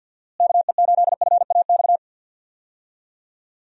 Morse code, heard as a keyed tone, about 50 wpm, 700 Hz. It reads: KE8FAX
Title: USA Call Signs - 6 Characters - Repeated 3x - 50wpm